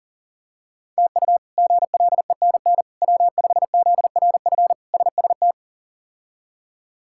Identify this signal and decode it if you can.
Morse code, heard as a keyed tone, about 40 wpm, 700 Hz. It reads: TU GLENN W5ZRF SST
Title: K1USN SST QSO Practice - 40wpm